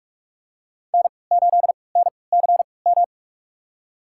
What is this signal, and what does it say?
Morse code, heard as a keyed tone, about 43 wpm, spaced as for 25 wpm, 700 Hz. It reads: N8NCM
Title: USA Call Signs - 5 Characters - 3x Character Spacing - 45wpm